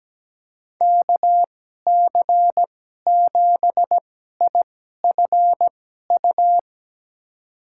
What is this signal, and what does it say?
Morse code, heard as a keyed tone, about 17 wpm, spaced as for 12 wpm, 700 Hz. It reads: KC7IFU
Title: USA Call Signs - 6 Characters - 2x Character Spacing - 17wpm